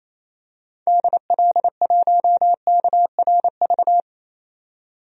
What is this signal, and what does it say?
Morse code, heard as a keyed tone, about 28 wpm, 700 Hz. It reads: DL1KRV
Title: DXCC-100 Call Signs - 6 Characters - 28wpm